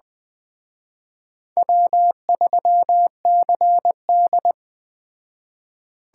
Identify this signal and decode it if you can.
Morse code, heard as a keyed tone, about 21 wpm, 700 Hz. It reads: W3CD